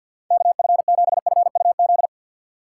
Morse code, heard as a keyed tone, about 53 wpm, 700 Hz. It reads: K F 7 F U Z
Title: USA Call Signs - 6 Characters - 6x Character Spacing - 50wpm